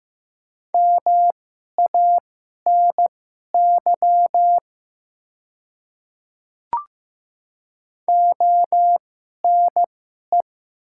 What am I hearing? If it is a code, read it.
Morse code, heard as a keyed tone, about 15 wpm, spaced as for 11 wpm, 690 Hz. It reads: MANY ONE
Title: 2x Character Spacing - Top 100 Words - 15wpm